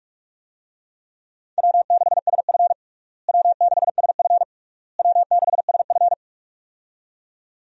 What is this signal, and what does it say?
Morse code, heard as a keyed tone, about 46 wpm, 700 Hz. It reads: W6SF W6SF W6SF
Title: USA Call Signs - 4 Characters - Repeated 3x - 45wpm